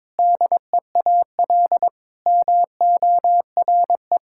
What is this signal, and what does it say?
Morse code, heard as a keyed tone, about 22 wpm, 700 Hz. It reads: DEAL MORE
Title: Sentences from Top 1000 Words - Review Part 6 - 22wpm